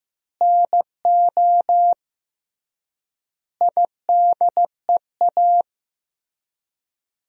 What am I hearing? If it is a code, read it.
Morse code, heard as a keyed tone, about 16 wpm, 700 Hz. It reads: NO IDEA
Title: 3x Word-Spacing - Sentences from Top 300 Words - 15wpm